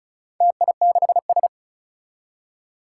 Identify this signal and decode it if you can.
Morse code, heard as a keyed tone, about 36 wpm, 700 Hz. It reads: TI6S
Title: Four Challenging Letter-Number Combinations 35wpm